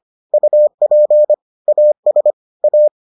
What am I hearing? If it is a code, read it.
Morse code, heard as a keyed tone, about 25 wpm, 600 Hz. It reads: UP AS A